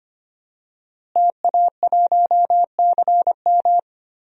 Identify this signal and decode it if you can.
Morse code, heard as a keyed tone, about 25 wpm, 700 Hz. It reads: TA1CM
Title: DXCC-100 Call Signs - 5 Characters - Repeated 2x - 25wpm